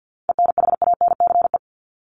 Morse code, heard as a keyed tone, about 50 wpm, 700 Hz. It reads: ENHANCE
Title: Top 1251-1500 Words 50wpm